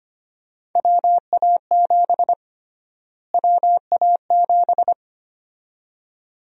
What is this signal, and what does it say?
Morse code, heard as a keyed tone, about 25 wpm, 700 Hz. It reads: WA7 WA7
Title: USA Call Sign Prefixes - Repeated 2x - 25wpm